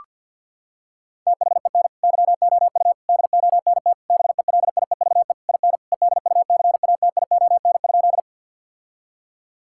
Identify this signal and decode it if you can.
Morse code, heard as a keyed tone, about 52 wpm, 700 Hz. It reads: THEN YOU DONT BELIEVE IN EDUCATION?
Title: Sentences from Top 1000 Words - Review Part 6 - 50wpm